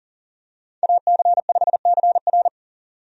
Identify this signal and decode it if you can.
Morse code, heard as a keyed tone, about 40 wpm, 700 Hz. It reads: AC5CR
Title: USA Call Signs - 5 Characters - 40wpm